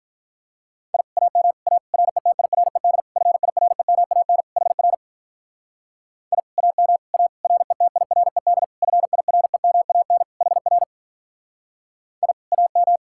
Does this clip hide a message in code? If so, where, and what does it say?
Yes: Morse code from 0.9 to 13.0 s, heard as a keyed tone, about 45 wpm, 700 Hz. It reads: I AM A RETIRED FIREMAN HR I AM A RETIRED FIREMAN HR I AM